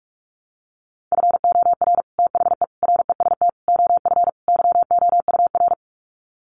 Morse code, heard as a keyed tone, about 44 wpm, 700 Hz. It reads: FOR THE REST OF YOUR
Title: Sets of 5 Words Encore 45wpm